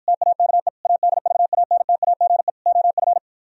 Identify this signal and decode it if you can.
Morse code, heard as a keyed tone, about 53 wpm, 700 Hz. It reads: TAKE ADVANTAGE OF